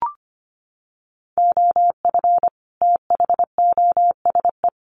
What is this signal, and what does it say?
Morse code, heard as a keyed tone, about 25 wpm, 700 Hz. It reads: OF THOSE
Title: Sets of 2 Words 25wpm